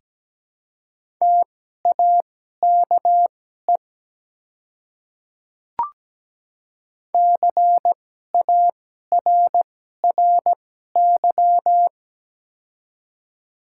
Morse code, heard as a keyed tone, about 17 wpm, spaced as for 12 wpm, 690 Hz. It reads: TAKE CARRY
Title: 2x Character Spacing - Top 300 Words - 17wpm